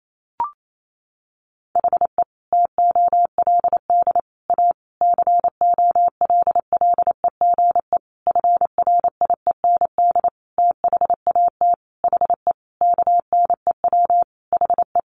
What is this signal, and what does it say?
Morse code, heard as a keyed tone, about 28 wpm, 700 Hz. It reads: HE TOLD A COLLEGE FRIEND THAT HE KNEW HE
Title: Sentences from Top 1300 Words 28wpm